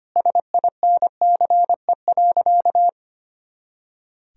Morse code, heard as a keyed tone, about 25 wpm, 700 Hz. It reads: SINCE.
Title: Sentences from Top 400 Words and Abbreviations 25wpm